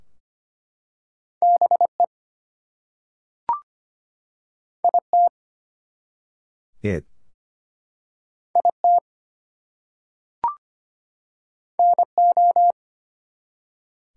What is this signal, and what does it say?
Morse code, heard as a keyed tone, about 25 wpm, 700 Hz. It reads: BE IT IT NO
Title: Two Letter Words - 25wpm